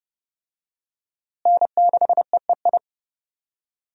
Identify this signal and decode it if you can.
Morse code, heard as a keyed tone, about 30 wpm, 700 Hz. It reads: N6EEI
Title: USA Call Signs - 5 Characters - Repeated 2x - 30wpm